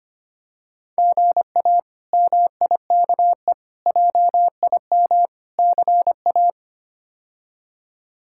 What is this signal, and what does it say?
Morse code, heard as a keyed tone, about 25 wpm, 700 Hz. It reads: GA MIKE JIM CA
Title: K1USN SST QSO Practice - 25wpm